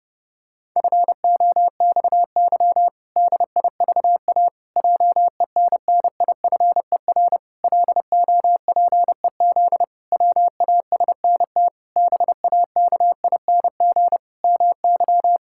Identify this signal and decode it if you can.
Morse code, heard as a keyed tone, about 32 wpm, 700 Hz. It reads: FOXY DIVA JENNIFER LOPEZ WASNT BAKING MY
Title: Pangram Challenge v2 - 30wpm